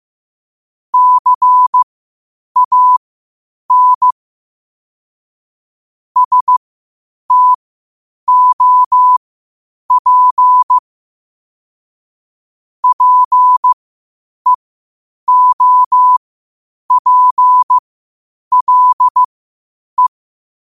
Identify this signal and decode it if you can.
Morse code, heard as a keyed tone, about 15 wpm, spaced as for 8 wpm, 1000 Hz. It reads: CAN STOP PEOPLE
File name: qotdfiles / 8wpm